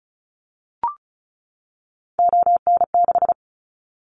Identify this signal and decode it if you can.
Morse code, heard as a keyed tone, about 34 wpm, 700 Hz. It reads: ON6
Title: Callsign Prefixes for Top 25 DXCC - 35wpm